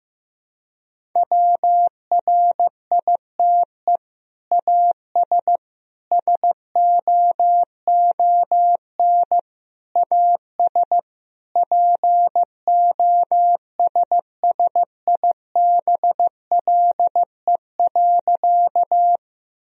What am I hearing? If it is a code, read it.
Morse code, heard as a keyed tone, about 16 wpm, 700 Hz. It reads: WRITE AS SOON AS POSSIBLE.
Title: Sentences from Top 1000 Words - Review Part 1 - 15wpm